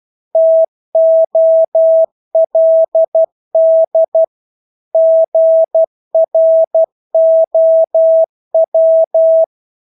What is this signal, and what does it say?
Morse code, heard as a keyed tone, about 12 wpm, 650 Hz. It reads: TOLD GROW
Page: Engelske ord 12 wpm